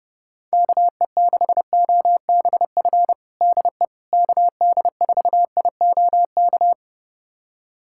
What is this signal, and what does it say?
Morse code, heard as a keyed tone, about 29 wpm, 700 Hz. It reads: KE6OBF DE KD4IOK
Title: USA Callsign DE Callsign 30wpm